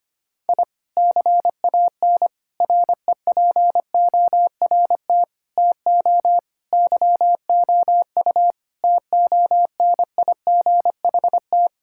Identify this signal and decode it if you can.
Morse code, heard as a keyed tone, about 25 wpm, 700 Hz. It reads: I CAN REPORT TO YOU TONIGHT
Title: Sentences from Top 1000 Words - Review Part 6 - 25wpm